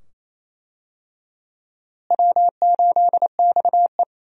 Morse code, heard as a keyed tone, about 27 wpm, 700 Hz. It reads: W8XE
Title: USA Call Signs - 4 Characters - Repeated 2x - 28wpm